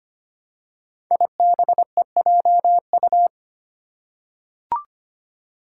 Morse code, heard as a keyed tone, about 25 wpm, 700 Hz. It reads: IBEJU
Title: Five Challenging Letter-Number Combinations 25wpm